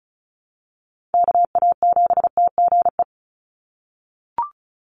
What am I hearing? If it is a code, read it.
Morse code, heard as a keyed tone, about 34 wpm, 700 Hz. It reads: KA7TGE
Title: USA Call Signs 35wpm